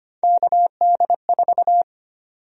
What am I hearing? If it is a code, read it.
Morse code, heard as a keyed tone, about 25 wpm, 700 Hz. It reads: KD4
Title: USA Call Sign Prefixes - 25wpm